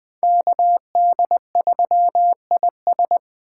Morse code, heard as a keyed tone, about 20 wpm, 700 Hz. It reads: KD3IS